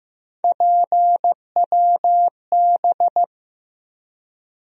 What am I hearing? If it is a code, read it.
Morse code, heard as a keyed tone, about 16 wpm, 700 Hz. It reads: PWB
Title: USA Call Sign Postfixes - 3 Letters 15wpm